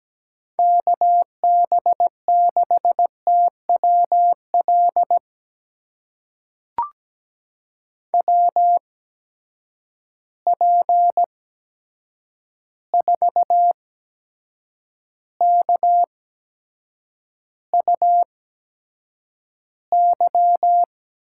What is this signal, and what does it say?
Morse code, heard as a keyed tone, about 17 wpm, spaced as for 7 wpm, 700 Hz. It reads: KB6TWL WP4KUY
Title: USA Call Signs - 6 Characters - 8x Character Spacing - 17wpm